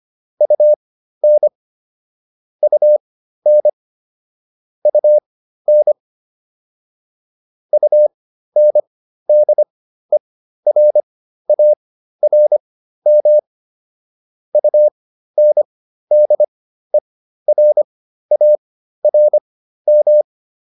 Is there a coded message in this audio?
Yes: Morse code from 0.4 to 20.2 s, heard as a keyed tone, about 25 wpm, spaced as for 13 wpm, 600 Hz. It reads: UN UN UN UNDERARM UNDERARM